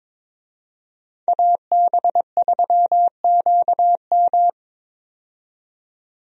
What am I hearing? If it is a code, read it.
Morse code, heard as a keyed tone, about 22 wpm, 700 Hz. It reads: AB3QM
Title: USA Call Signs - 5 Characters - 22wpm